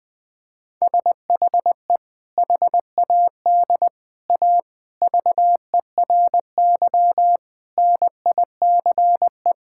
Morse code, heard as a keyed tone, about 21 wpm, 700 Hz. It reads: SHE HAD A VERY NICE
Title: Sentences with 2-Word Phrases - Spoken Beginning - 20wpm